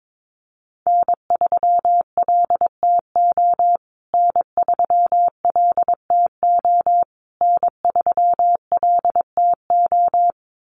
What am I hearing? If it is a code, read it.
Morse code, heard as a keyed tone, about 22 wpm, 700 Hz. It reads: N3LTO N3LTO N3LTO
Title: USA Call Signs Repeated 3x 22wpm